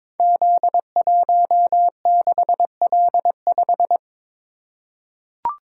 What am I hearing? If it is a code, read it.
Morse code, heard as a keyed tone, about 22 wpm, 700 Hz. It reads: Z16L5
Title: Code Groups - 5 Letter:Numbers - 22wpm